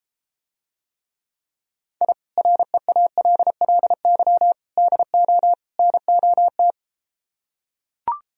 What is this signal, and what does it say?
Morse code, heard as a keyed tone, about 33 wpm, 700 Hz. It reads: I REALLY DO NOT